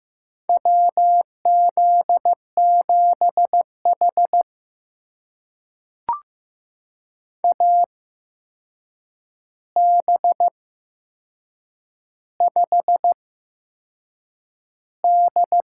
Morse code, heard as a keyed tone, about 15 wpm, spaced as for 6 wpm, 700 Hz. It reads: WZ7H AB5D
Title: USA Call Signs - 4 Characters - 8x Character Spacing - 15wpm